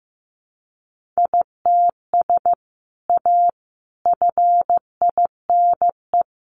Morse code, heard as a keyed tone, about 15 wpm, 700 Hz. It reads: ITS A FINE
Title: Sentences from Top 600 Words 15wpm